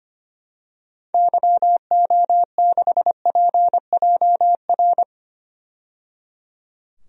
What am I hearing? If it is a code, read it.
Morse code, heard as a keyed tone, about 25 wpm, 700 Hz. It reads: YO6PJR
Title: DXCC-100 Call Signs - 6 Characters - 25wpm